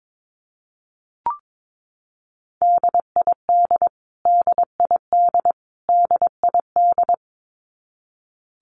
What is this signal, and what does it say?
Morse code, heard as a keyed tone, about 22 wpm, 700 Hz. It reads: DID DID DID
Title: Top 200 Words Repeated 22wpm